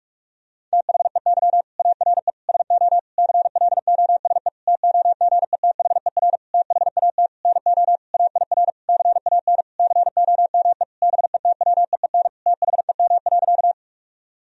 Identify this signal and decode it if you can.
Morse code, heard as a keyed tone, about 46 wpm, 700 Hz. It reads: THEY ARE SO CLOSE TOGETHER THAT NO AIR CAN COME BETWEEN THEM.
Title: Sentences from Top 1000 Words - Review Part 1 - 45wpm